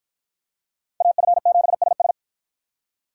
Morse code, heard as a keyed tone, about 47 wpm, 700 Hz. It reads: AF7SS